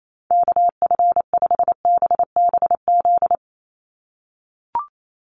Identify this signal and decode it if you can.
Morse code, heard as a keyed tone, about 28 wpm, 700 Hz. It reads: KF5BBZ
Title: USA Call Signs 28wpm